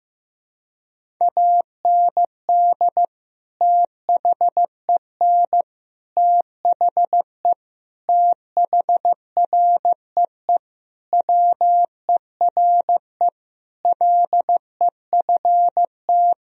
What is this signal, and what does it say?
Morse code, heard as a keyed tone, about 14 wpm, 700 Hz. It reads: AND THEN THE THREE WERE LEFT
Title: Sentences from Top 1000 Words - Review Part 1 - 15wpm